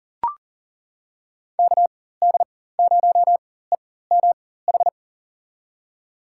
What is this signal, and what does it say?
Morse code, heard as a keyed tone, about 40 wpm, spaced as for 19 wpm, 700 Hz. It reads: KD0EMH
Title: USA Call Signs - 4x Character Spacing - 40wpm